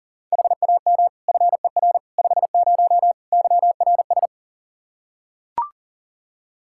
Morse code, heard as a keyed tone, about 40 wpm, 700 Hz. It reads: HAM FER 50 YRS
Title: Practice QSO - Exchange 3 - More Info - 40wpm